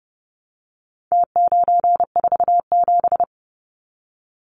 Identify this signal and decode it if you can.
Morse code, heard as a keyed tone, about 30 wpm, 700 Hz. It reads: T947
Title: Numbers - Contest Style - 4-Digit 30wpm